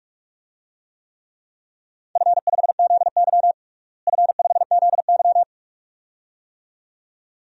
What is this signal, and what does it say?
Morse code, heard as a keyed tone, about 46 wpm, 700 Hz. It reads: F5ZY F5ZY
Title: DXCC-100 Call Signs - 4 Characters - Repeated 2x - 45wpm